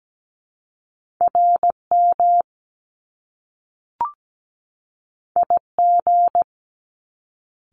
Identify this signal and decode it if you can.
Morse code, heard as a keyed tone, about 17 wpm, 700 Hz. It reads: RM IG
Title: Top Two Letter Combinations in Words 17wpm